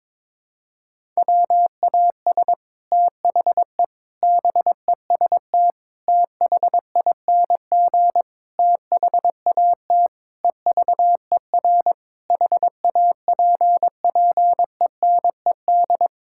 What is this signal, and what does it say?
Morse code, heard as a keyed tone, about 22 wpm, 700 Hz. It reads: WAS THE BEST THING THAT EVER HAPPENED
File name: practice / 22wpm / Sets of 8 Words - 22wpm